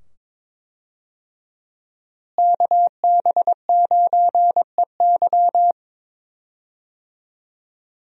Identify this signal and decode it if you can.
Morse code, heard as a keyed tone, about 21 wpm, 700 Hz. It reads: KB9EY
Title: USA Call Signs - 5 Characters - Repeated 2x - 22wpm